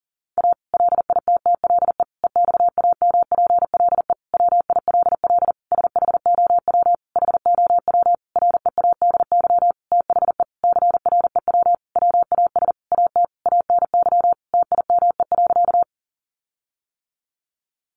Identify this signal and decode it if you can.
Morse code, heard as a keyed tone, about 40 wpm, 700 Hz. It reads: A LITTLE EXAMPLE WILL SHOW HOW READY THE CREW WAS AT ANY TIME.
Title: Sentences from Top 1500 Words 40wpm